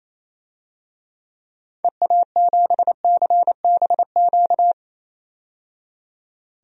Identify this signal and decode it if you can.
Morse code, heard as a keyed tone, about 28 wpm, 700 Hz. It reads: EA7CBQ
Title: DXCC-100 Call Signs - 6 Characters - 28wpm